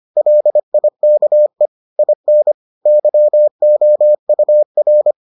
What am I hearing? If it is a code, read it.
Morse code, heard as a keyed tone, about 25 wpm, 600 Hz. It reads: LIKE IN YOUR